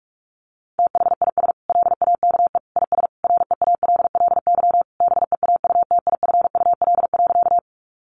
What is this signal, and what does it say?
Morse code, heard as a keyed tone, about 44 wpm, 700 Hz. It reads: THIS LAKE IS REALLY BEAUTIFUL.
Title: Sentences from Top 1100 Words 45wpm